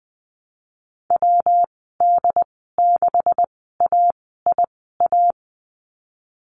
Morse code, heard as a keyed tone, about 20 wpm, spaced as for 15 wpm, 700 Hz. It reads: WD6AIA
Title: 193 Lesson 31g - Letter v - Call Signs 20wpm